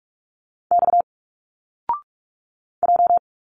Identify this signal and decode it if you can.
Morse code, heard as a keyed tone, about 45 wpm, 700 Hz. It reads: X J
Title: Single Letters 45wpm